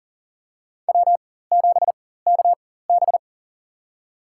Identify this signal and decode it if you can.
Morse code, heard as a keyed tone, about 40 wpm, spaced as for 19 wpm, 700 Hz. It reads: W7KB
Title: USA Call Signs - 4 Characters - 4x Character Spacing - 40wpm